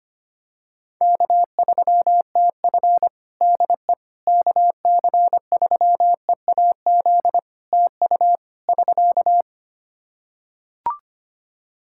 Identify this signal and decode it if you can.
Morse code, heard as a keyed tone, about 25 wpm, 700 Hz. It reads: K3TF DE KC3EAZ TU <SK>
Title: Practice QSO - Exchange 4 - Ending - 25wpm